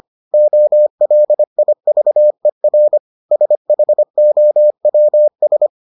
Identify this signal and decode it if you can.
Morse code, heard as a keyed tone, about 25 wpm, 600 Hz. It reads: OLIVER SHOWS